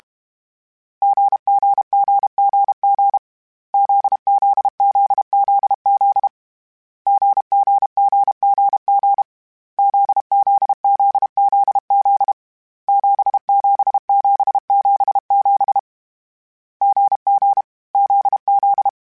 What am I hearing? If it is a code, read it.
Morse code, heard as a keyed tone, about 32 wpm, 800 Hz. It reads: GGGGG ZZZZZ GGGGG ZZZZZ 77777 GG ZZ
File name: Short Runs / Short Runs-mp3s / GZ GZ7 @30